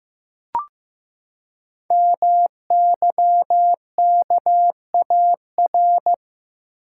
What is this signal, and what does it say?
Morse code, heard as a keyed tone, about 15 wpm, 700 Hz. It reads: MYKAR
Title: Code Groups - 5 Letters - 15wpm